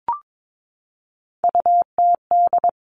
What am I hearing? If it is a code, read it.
Morse code, heard as a keyed tone, about 22 wpm, 700 Hz. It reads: UTD
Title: Three Challenging Letter-Number Combinations 22wpm